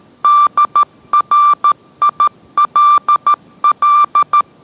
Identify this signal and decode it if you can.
Morse code, heard as a keyed tone, about 13 wpm, 1180 Hz. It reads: DRILL